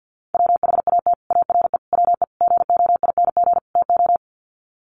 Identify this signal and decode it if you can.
Morse code, heard as a keyed tone, about 48 wpm, 700 Hz. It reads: WHAT ARE WE GOING TO
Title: Sets of 5 Words 50wpm